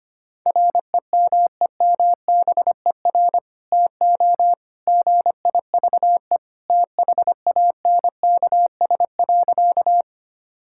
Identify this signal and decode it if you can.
Morse code, heard as a keyed tone, about 25 wpm, 700 Hz. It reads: REMEMBER TO GIVE THANKS.
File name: practice / 25wpm / Sentences with 2-Word Phrases - 25wpm